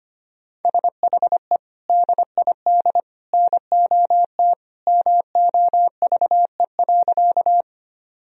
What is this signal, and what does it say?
Morse code, heard as a keyed tone, about 25 wpm, 700 Hz. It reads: SHE DID NOT MOVE.
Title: Sentences from Top 1000 Words - Review Part 1 - 25wpm